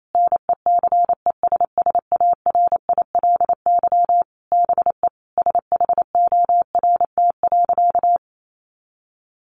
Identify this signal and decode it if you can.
Morse code, heard as a keyed tone, about 28 wpm, 700 Hz. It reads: NECESSARILY BE SHORT.
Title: Sentences from Top 1400 Words 28wpm